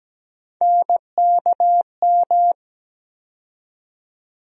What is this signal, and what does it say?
Morse code, heard as a keyed tone, about 17 wpm, 700 Hz. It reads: NKM
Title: USA Call Sign Postfixes - 3 Letters 17wpm